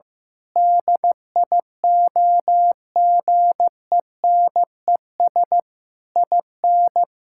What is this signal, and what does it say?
Morse code, heard as a keyed tone, about 14 wpm, 700 Hz. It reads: DIOGENES IN
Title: Pangram Challenge v2 - 15wpm